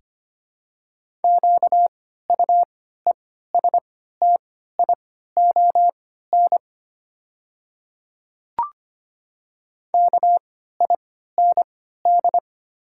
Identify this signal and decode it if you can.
Morse code, heard as a keyed tone, about 25 wpm, spaced as for 14 wpm, 690 Hz. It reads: QUESTION KIND
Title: 3x Character Spacing - Top 500 Words - 25wpm